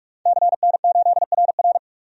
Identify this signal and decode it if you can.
Morse code, heard as a keyed tone, about 41 wpm, 700 Hz. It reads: CN8RR
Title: DXCC-100 Call Signs - 5 Characters - 45wpm